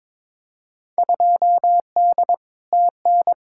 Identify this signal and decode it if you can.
Morse code, heard as a keyed tone, about 22 wpm, 700 Hz. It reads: 2D TN
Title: Field Day Exchanges Encore - 22wpm